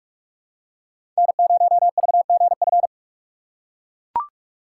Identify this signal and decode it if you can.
Morse code, heard as a keyed tone, about 42 wpm, 700 Hz. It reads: N0VGF
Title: USA Call Signs - 4x Character Spacing - 45wpm